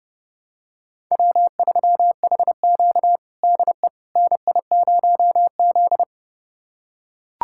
Morse code, heard as a keyed tone, about 30 wpm, 700 Hz. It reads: W3HQ DE NI0Z
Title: USA Callsign DE Callsign - 4 Characters - 30wpm